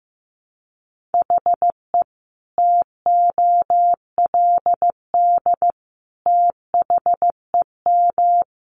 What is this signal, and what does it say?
Morse code, heard as a keyed tone, about 15 wpm, 700 Hz. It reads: HE TOLD THEM
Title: Sentences from Top 700 Words 15wpm